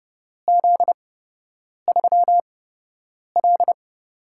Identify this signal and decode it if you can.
Morse code, heard as a keyed tone, about 30 wpm, spaced as for 8 wpm, 700 Hz. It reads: Z3L
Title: USA Call Signs - 4 Characters - 8x Character Spacing - 30wpm